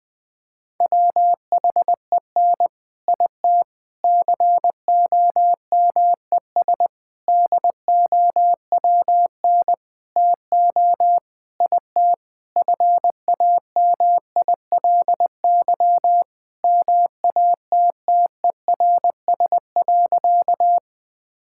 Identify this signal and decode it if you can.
Morse code, heard as a keyed tone, about 21 wpm, 700 Hz. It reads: WHEN IT COMES DOWN TO IT FAMILY MATTERS.
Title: Sentences with 6-Word Phrases - 20wpm